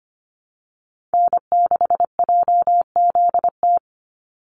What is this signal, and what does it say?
Morse code, heard as a keyed tone, about 25 wpm, 700 Hz. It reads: N6JZT
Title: NA SOTA Chaser Callsigns 25wpm